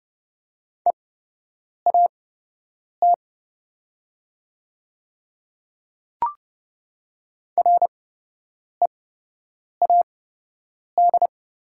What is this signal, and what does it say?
Morse code, heard as a keyed tone, about 30 wpm, spaced as for 8 wpm, 690 Hz. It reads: EAT READ
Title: 8x Character Spacing - Top 300 Words - 30wpm